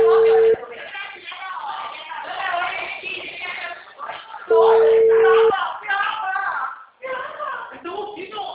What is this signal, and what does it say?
Ringback tone (China, 450 Hz), with speech.
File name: out- -2016- - - .57902